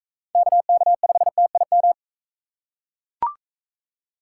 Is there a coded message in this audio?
Yes: Morse code from 0.4 to 1.9 s, heard as a keyed tone, about 42 wpm, 700 Hz. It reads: KK5TIM